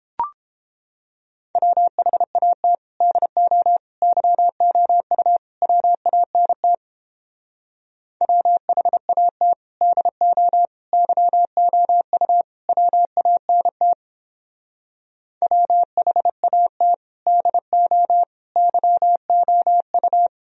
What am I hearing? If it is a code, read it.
Morse code, heard as a keyed tone, about 31 wpm, 700 Hz. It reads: WHAT DO YOU WANT WHAT DO YOU WANT WHAT DO YOU